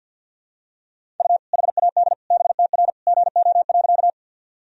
Morse code, heard as a keyed tone, about 47 wpm, 700 Hz. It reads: U HAD BTR GO.
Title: Sentences from Top 400 Words and Abbreviations 50wpm